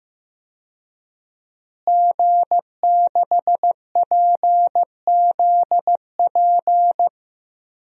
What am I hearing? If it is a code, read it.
Morse code, heard as a keyed tone, about 15 wpm, 700 Hz. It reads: G6PZP
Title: DXCC-100 Call Signs - 5 Characters - Repeated 2x - 15wpm